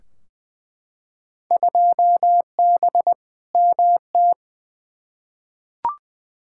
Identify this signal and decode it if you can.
Morse code, heard as a keyed tone, about 20 wpm, 700 Hz. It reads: 2B MT
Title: Field Day Exchanges Encore - 20wpm